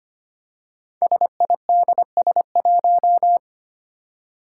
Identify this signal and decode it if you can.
Morse code, heard as a keyed tone, about 25 wpm, 700 Hz. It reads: SIDS1
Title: Five Challenging Letter-Number Combinations 25wpm